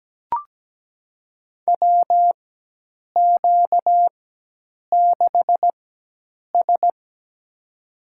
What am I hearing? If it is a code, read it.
Morse code, heard as a keyed tone, about 17 wpm, spaced as for 8 wpm, 700 Hz. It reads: WQ6S
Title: USA Call Signs - 4x Character Spacing - 17wpm